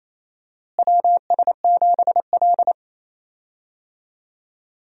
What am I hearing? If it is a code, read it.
Morse code, heard as a keyed tone, about 27 wpm, 700 Hz. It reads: WS7L
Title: USA Call Signs - 4 Characters - Repeated 2x - 28wpm